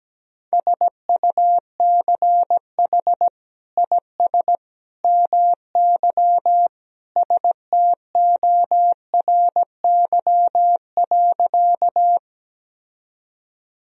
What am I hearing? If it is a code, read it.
Morse code, heard as a keyed tone, about 17 wpm, 700 Hz. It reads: SUCH IS MY STORY.
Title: Sentences from Top 1000 Words - Review Part 1 - 17wpm